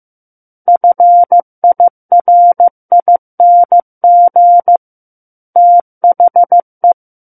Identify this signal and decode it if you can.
Morse code, heard as a keyed tone, about 15 wpm, 700 Hz. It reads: FIRING THE
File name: Chapter 6 - 15 WPM